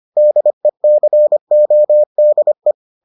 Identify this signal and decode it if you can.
Morse code, heard as a keyed tone, about 25 wpm, 600 Hz. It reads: DECODE